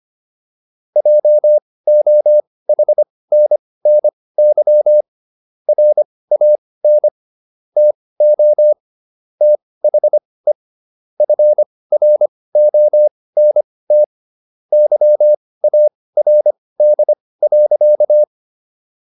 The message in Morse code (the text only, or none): JOHNNY RAN TO THE FRONT YARD.